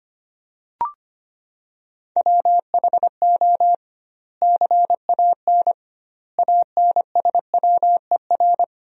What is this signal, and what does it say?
Morse code, heard as a keyed tone, about 25 wpm, 700 Hz. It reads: WHO CAN ANSWER
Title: 2x Word-Spacing - Sentences from Top 300 Words - 25wpm